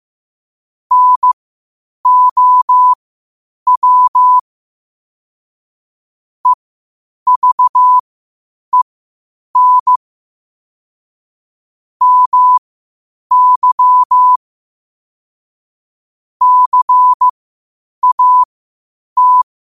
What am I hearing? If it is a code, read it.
Morse code, heard as a keyed tone, about 15 wpm, spaced as for 8 wpm, 1000 Hz. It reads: NOW EVEN MY CAT
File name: qotdfiles / 8wpm